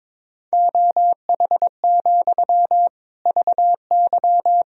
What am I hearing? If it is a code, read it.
Morse code, heard as a keyed tone, about 23 wpm, 700 Hz. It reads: OH, VY
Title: Sentences from Top 500 Words and Abbreviations 22wpm